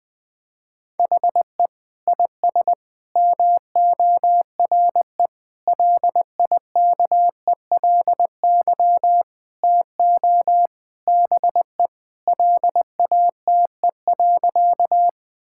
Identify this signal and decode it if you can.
Morse code, heard as a keyed tone, about 20 wpm, 700 Hz. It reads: HE IS MORE LIKELY TO BE LATE.
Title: Sentences with 4-Word Phrases - 20wpm